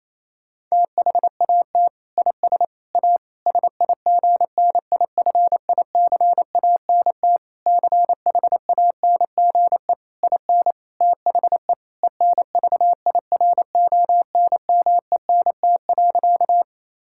Morse code, heard as a keyed tone, about 28 wpm, 700 Hz. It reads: THAT IS A SIGNIFICANT CHANGE IN THE ENVIRONMENT.
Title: Sentences from Top 1000 Words - Review Part 6 - 28wpm